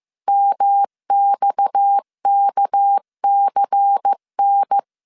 Morse code, heard as a keyed tone, about 15 wpm, 800 Hz. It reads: MXKCN
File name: CW / DET / 1 9 1500C S A 14wpm 3222